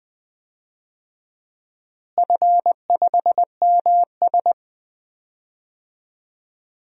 Morse code, heard as a keyed tone, about 21 wpm, 700 Hz. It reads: F5MS
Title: DXCC-100 Call Signs - 4 Characters - Repeated 3x - 20wpm